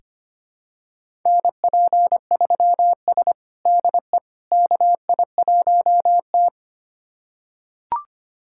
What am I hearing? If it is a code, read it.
Morse code, heard as a keyed tone, about 25 wpm, 700 Hz. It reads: NP3S DE KI1T
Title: USA Callsign DE Callsign - 4 Characters - 25wpm